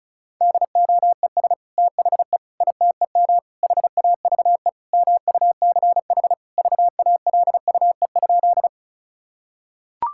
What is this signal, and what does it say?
Morse code, heard as a keyed tone, about 34 wpm, 700 Hz. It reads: DOES THE ITEM HAVE MUCH VALUE?
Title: Sentences from Top 1000 Words - Review Part 6 - 35wpm